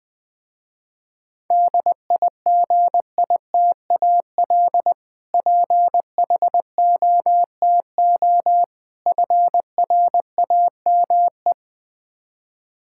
Morse code, heard as a keyed tone, about 20 wpm, 700 Hz. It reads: DIGITAL PHOTO FRAME
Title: Difficult Words with Hints - 20wpm